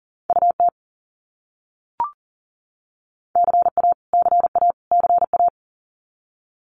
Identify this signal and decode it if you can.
Morse code, heard as a keyed tone, about 40 wpm, 700 Hz. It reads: UT CA CA CA
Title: US States Repeated 3x 40wpm